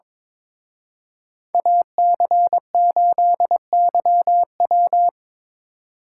Morse code, heard as a keyed tone, about 25 wpm, 700 Hz. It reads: AC8YW
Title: Speed Racing - USA Call Signs 22wpm